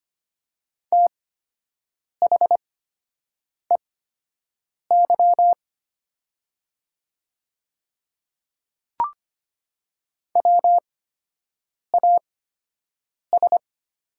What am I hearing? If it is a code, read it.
Morse code, heard as a keyed tone, about 25 wpm, spaced as for 7 wpm, 700 Hz. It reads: THEY WAS
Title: 8x Character Spacing - Top 100 Words - 25wpm